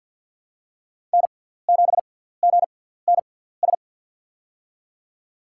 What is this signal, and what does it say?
Morse code, heard as a keyed tone, about 48 wpm, spaced as for 17 wpm, 700 Hz. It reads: N7GNS
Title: USA Call Signs - 5 Characters - 6x Character Spacing - 50wpm